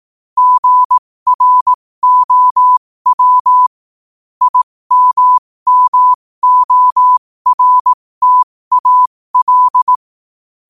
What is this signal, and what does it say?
Morse code, heard as a keyed tone, about 18 wpm, spaced as for 15 wpm, 1000 Hz. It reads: GROW IMMORTAL
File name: qotdfiles / 15wpm